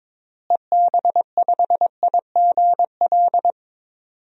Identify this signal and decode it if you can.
Morse code, heard as a keyed tone, about 22 wpm, 700 Hz. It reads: EB5IGL
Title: DXCC-100 Call Signs - 6 Characters - Repeated 3x - 22wpm